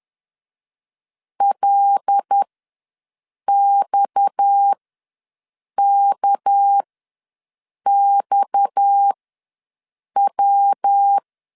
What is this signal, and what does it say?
Morse code, heard as a keyed tone, about 10 wpm, spaced as for 6 wpm, 800 Hz. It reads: LXKXW
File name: LEZIONI / 6 PKX0 / 6 PKX ws 10wpm 3706